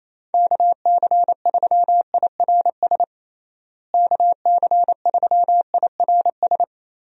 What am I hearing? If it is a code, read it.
Morse code, heard as a keyed tone, about 27 wpm, 700 Hz. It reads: KC3IRS KC3IRS
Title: USA Call Signs - 6 Characters - Repeated 3x - 28wpm